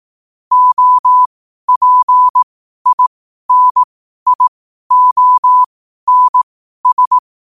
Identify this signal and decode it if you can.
Morse code, heard as a keyed tone, about 18 wpm, spaced as for 12 wpm, 1000 Hz. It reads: OPINIONS
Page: Quotes for Wed, 13 Aug 2025 in Morse Code at 12 words per minute.